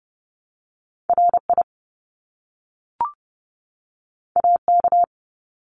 Morse code, heard as a keyed tone, about 30 wpm, 700 Hz. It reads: RI AK
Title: US States 30wpm